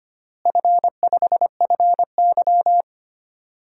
Morse code, heard as a keyed tone, about 25 wpm, 700 Hz. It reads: F5FY
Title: DXCC-100 Call Signs - 4 Characters - Repeated 2x - 25wpm